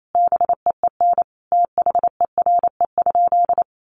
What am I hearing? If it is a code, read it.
Morse code, heard as a keyed tone, about 28 wpm, 700 Hz. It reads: BEEN THERE?
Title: Sentences from Top 300 Words 28wpm